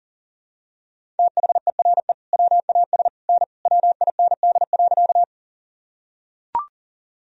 Morse code, heard as a keyed tone, about 40 wpm, 700 Hz. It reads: THERE WAS N WIND.
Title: Sentences from Top 400 Words and Abbreviations 40wpm